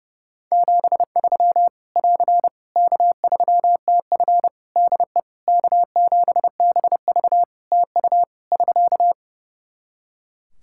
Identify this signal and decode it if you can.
Morse code, heard as a keyed tone, about 32 wpm, 700 Hz. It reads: 73 <AR> K3TF DE K7BV TU <SK>
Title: Practice QSO - Exchange 4 - Ending - 30wpm